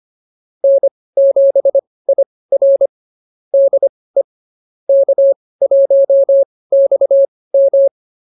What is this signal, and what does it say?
Morse code, heard as a keyed tone, about 24 wpm, spaced as for 18 wpm, 550 Hz. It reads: N7IR DE K1XM